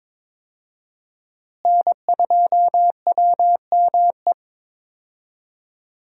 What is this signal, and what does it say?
Morse code, heard as a keyed tone, about 21 wpm, 700 Hz. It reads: N2WME
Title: USA Call Signs - 5 Characters - Repeated 2x - 22wpm